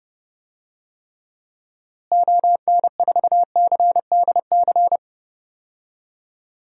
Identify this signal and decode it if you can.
Morse code, heard as a keyed tone, about 32 wpm, 700 Hz. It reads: ON4CDC
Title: DXCC-100 Call Signs - 6 Characters - Repeated 3x - 30wpm